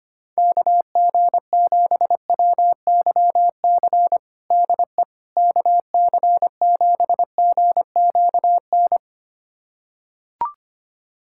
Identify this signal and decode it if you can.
Morse code, heard as a keyed tone, about 25 wpm, 700 Hz. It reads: KG7WYC DE KC7GQN
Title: USA Callsign DE Callsign 25wpm